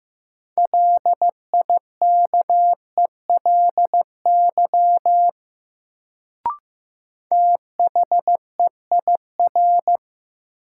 Morse code, heard as a keyed tone, about 15 wpm, 700 Hz. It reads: LIKELY THEIR
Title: Sets of 2 Words - New Word Drill - 15wpm